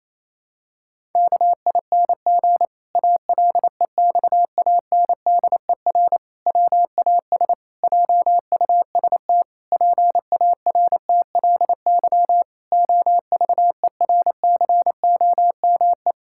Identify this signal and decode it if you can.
Morse code, heard as a keyed tone, about 27 wpm, 700 Hz. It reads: KING ALEXANDER WAS JUST PARTLY OVERCOME
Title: Pangram Challenge v2 - 28wpm